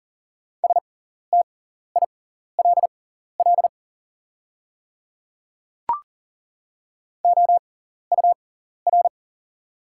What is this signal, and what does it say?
Morse code, heard as a keyed tone, about 40 wpm, spaced as for 14 wpm, 700 Hz. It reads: STILL OUR
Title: 6x Character Spacing - Top 300 Words - 40wpm